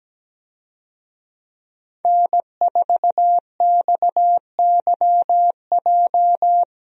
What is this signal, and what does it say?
Morse code, heard as a keyed tone, about 17 wpm, 700 Hz. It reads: N4XYJ
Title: USA Call Signs - 5 Characters - 17wpm